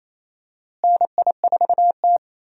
Morse code, heard as a keyed tone, about 28 wpm, 700 Hz. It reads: NI4T
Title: USA Call Signs - 4 Characters - Repeated 2x - 28wpm